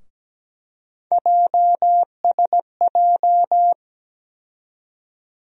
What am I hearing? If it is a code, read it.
Morse code, heard as a keyed tone, about 17 wpm, 700 Hz. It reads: JSJ
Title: USA Call Sign Postfixes - 3 Letters 17wpm